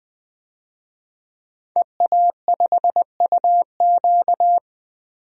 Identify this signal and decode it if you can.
Morse code, heard as a keyed tone, about 20 wpm, 700 Hz. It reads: EA5UQ
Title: DXCC-100 Call Signs - 5 Characters - Repeated 2x - 20wpm